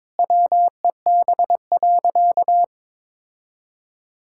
Morse code, heard as a keyed tone, about 22 wpm, 700 Hz. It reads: WEB.
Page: morse-code-ninja-practice-22wpm